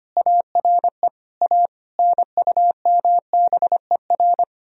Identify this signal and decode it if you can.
Morse code, heard as a keyed tone, about 31 wpm, 700 Hz. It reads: ARE A NUMBER
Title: Speed-Racing - Sets of 5 Words 25wpm